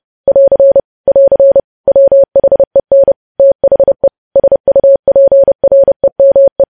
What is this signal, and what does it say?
Morse code, heard as a keyed tone, about 30 wpm, 550 Hz. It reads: <AR> <AR> WHEN THE SUPREME
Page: News Headlines in Morse Code at 30 WPM